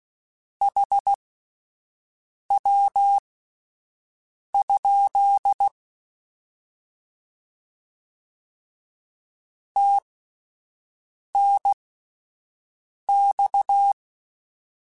Morse code, heard as a keyed tone, about 16 wpm, spaced as for 5 wpm, 800 Hz. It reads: HW? TNX
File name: Abreviations / Abreviations mp3s / Abreviations@5